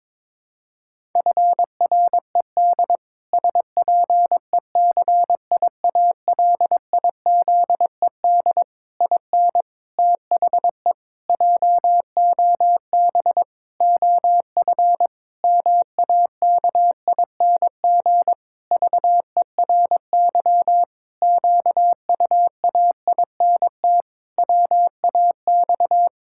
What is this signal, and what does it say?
Morse code, heard as a keyed tone, about 22 wpm, 700 Hz. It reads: FRED SPECIALIZED IN THE JOB OF MAKING VERY QUAINT WAX
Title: Pangram Challenge v2 - 22wpm